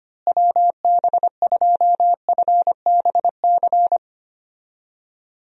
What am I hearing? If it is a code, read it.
Morse code, heard as a keyed tone, about 25 wpm, 700 Hz. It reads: WB2FBC
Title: USA Call Signs - 6 Characters - Repeated 3x - 25wpm